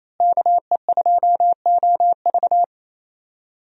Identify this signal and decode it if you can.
Morse code, heard as a keyed tone, about 28 wpm, 700 Hz. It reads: KE2OV
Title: USA Call Signs - 5 Characters - Repeated 3x - 28wpm